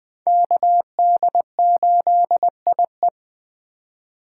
Morse code, heard as a keyed tone, about 20 wpm, 700 Hz. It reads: KD8IE
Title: 2023 POTA Activator Call Signs - 20wpm